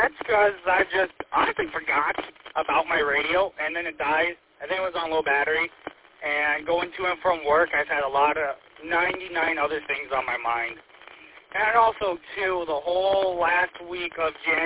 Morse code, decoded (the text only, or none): none